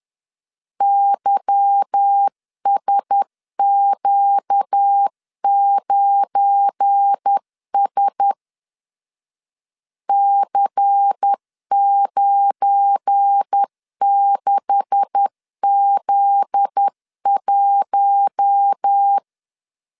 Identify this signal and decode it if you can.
Morse code, heard as a keyed tone, about 10 wpm, 800 Hz. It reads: YSQ9S C96Z1
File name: LEZIONI / 9 286 10wpm 2735